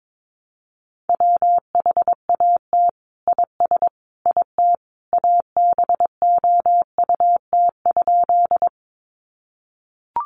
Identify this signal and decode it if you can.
Morse code, heard as a keyed tone, about 22 wpm, 700 Hz. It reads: WHAT IS IT ABOUT?
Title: Sentences from Top 100 Words 22wpm